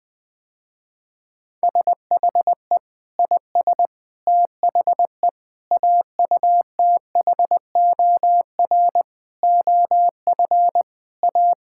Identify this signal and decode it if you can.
Morse code, heard as a keyed tone, about 20 wpm, 700 Hz. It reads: SHE IS THE AUTHOR OF A
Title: Sentences with 4-Word Phrases - Spoken Beginning - 20wpm